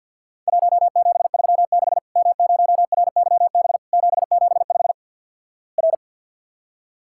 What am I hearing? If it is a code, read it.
Morse code, heard as a keyed tone, about 50 wpm, 700 Hz. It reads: 1736 M0RYB 775 R
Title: Worked All Europe DX Contest - QTC Exchange - 50wpm